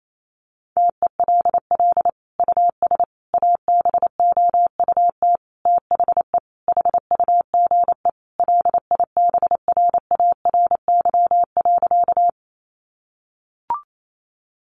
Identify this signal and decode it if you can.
Morse code, heard as a keyed tone, about 28 wpm, 700 Hz. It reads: TELL US ABOUT THE HUGE LIBRARY.
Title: Sentences from Top 1100 Words 28wpm